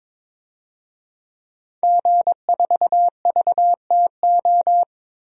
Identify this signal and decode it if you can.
Morse code, heard as a keyed tone, about 22 wpm, 700 Hz. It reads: G4VTO
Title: DXCC-100 Call Signs - 5 Characters - Repeated 3x - 22wpm